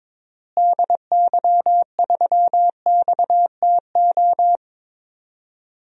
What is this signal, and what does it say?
Morse code, heard as a keyed tone, about 22 wpm, 700 Hz. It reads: DY3XTO
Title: DXCC-100 Call Signs - 6 Characters - Repeated 3x - 22wpm